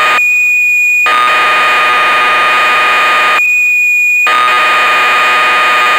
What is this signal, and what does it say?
Cassette data (Kansas City type).
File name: magazines / Acorn User / 1985-06 BBC Programs